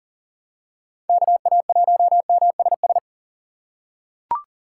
Morse code, heard as a keyed tone, about 40 wpm, 700 Hz. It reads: KA1MSS
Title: USA Call Signs - 6 Characters - 4x Character Spacing - 40wpm